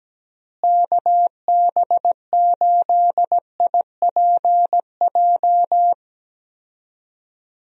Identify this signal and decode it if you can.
Morse code, heard as a keyed tone, about 17 wpm, 700 Hz. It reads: KB8IPJ
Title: USA Call Signs - 6 Characters - Repeated 3x - 17wpm